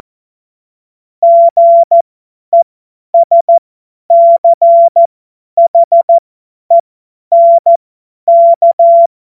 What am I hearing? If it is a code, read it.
Morse code, heard as a keyed tone, about 14 wpm, spaced as for 10 wpm, 680 Hz. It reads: GESCHENK
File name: Morsekurs / Witze / Witz 31. Januar 2023 50 F70